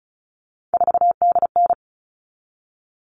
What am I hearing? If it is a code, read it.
Morse code, heard as a keyed tone, about 34 wpm, 700 Hz. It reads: 4DN
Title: All Cut Numbers - 3-Digit 35wpm